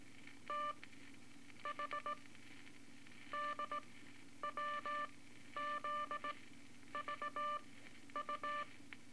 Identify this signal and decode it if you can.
Morse code, heard as a keyed tone, about 19 wpm, spaced as for 11 wpm, 1180 Hz, with noise. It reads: T H D WZ VU
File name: Morse Code Records / 18 A LCM 164 Ten Words Per Minute